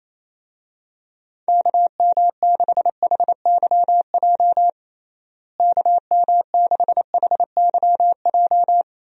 Morse code, heard as a keyed tone, about 27 wpm, 700 Hz. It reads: KM6HYJ KM6HYJ
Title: USA Call Signs - 6 Characters - Repeated 3x - 28wpm